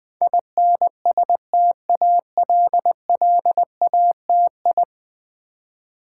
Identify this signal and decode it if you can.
Morse code, heard as a keyed tone, about 20 wpm, 700 Hz. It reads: INSTALLATI
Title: Build A Word - 5 Letters At A Time - 12-Letter Words - 20wpm